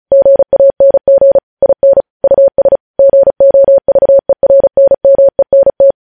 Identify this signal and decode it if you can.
Morse code, heard as a keyed tone, about 36 wpm, 550 Hz. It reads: GANG IN US GOVERNMENT
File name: podcast / News headlines-1x-35x35